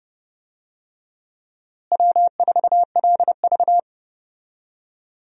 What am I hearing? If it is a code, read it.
Morse code, heard as a keyed tone, about 30 wpm, 700 Hz. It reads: W4LV
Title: USA Call Signs - 4 Characters - Repeated 2x - 30wpm